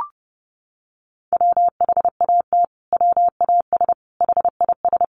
Morse code, heard as a keyed tone, about 30 wpm, 700 Hz. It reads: WHAT WAS HIS
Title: Sentences from Top 1400 Words 30wpm